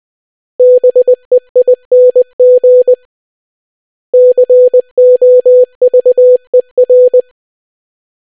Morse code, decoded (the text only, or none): BEING COVER